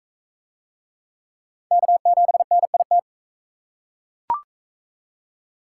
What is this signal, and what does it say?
Morse code, heard as a keyed tone, about 37 wpm, 700 Hz. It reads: K7NIT